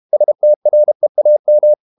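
Morse code, heard as a keyed tone, about 32 wpm, 600 Hz. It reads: STREAM
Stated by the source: Morse code practice at 32 wpm from CW med Gnister, reading Ken Follett's novel Never in English.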